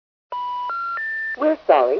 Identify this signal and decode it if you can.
Special information tone, with speech.